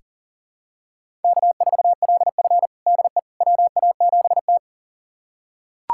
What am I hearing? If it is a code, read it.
Morse code, heard as a keyed tone, about 40 wpm, 700 Hz. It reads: K4LF DE WA7T
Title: USA Callsign DE Callsign - 4 Characters - 40wpm